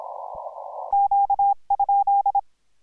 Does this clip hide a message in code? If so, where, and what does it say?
Yes: Morse code from 0.9 to 2.4 s, heard as a keyed tone, about 27 wpm, 800 Hz. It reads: Q?